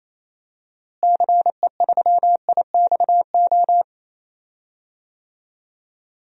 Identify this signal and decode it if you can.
Morse code, heard as a keyed tone, about 28 wpm, 700 Hz. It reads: CE3IXO
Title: DXCC-100 Call Signs - 6 Characters - Repeated 2x - 28wpm